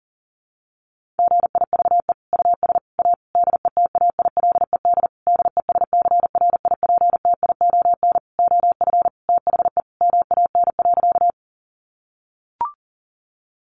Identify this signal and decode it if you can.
Morse code, heard as a keyed tone, about 40 wpm, 700 Hz. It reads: GIVE US A DETAILED DESCRIPTION OF THE MAN.
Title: Sentences from Top 1500 Words 40wpm